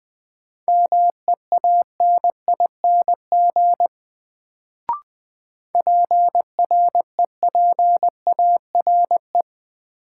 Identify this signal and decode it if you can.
Morse code, heard as a keyed tone, about 20 wpm, 700 Hz. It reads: MEANING PREPARE
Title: Sentences from Top 900 Words - New Word Drill - 20wpm